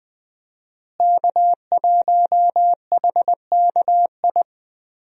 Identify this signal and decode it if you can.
Morse code, heard as a keyed tone, about 20 wpm, 700 Hz. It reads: K1HKI
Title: USA Call Signs - 5 Characters - Repeated 2x - 20wpm